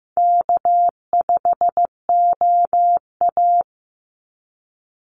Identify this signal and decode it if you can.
Morse code, heard as a keyed tone, about 15 wpm, 700 Hz. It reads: K5OA
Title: Call Signs Repeated 2x - 15wpm